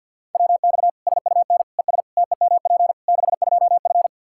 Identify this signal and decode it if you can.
Morse code, heard as a keyed tone, about 52 wpm, 700 Hz. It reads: WX SUN ES TEMP 62F
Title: Practice QSO - Exchange 2 - Info - 50wpm